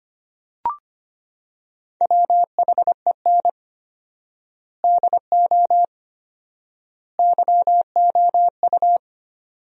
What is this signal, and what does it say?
Morse code, heard as a keyed tone, about 25 wpm, 700 Hz. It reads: WHEN DO YOU
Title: 4x Word-Spacing - Sentences from Top 400 Words - 25wpm